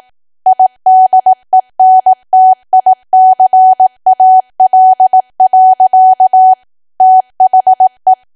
Morse code, heard as a keyed tone, about 18 wpm, 750 Hz. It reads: IDENTICAL. THE
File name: cwa / W1AW-Level2 / 18WPM